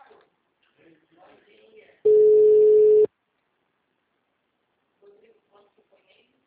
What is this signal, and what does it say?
Ringback tone (Europe, 425 Hz), with speech.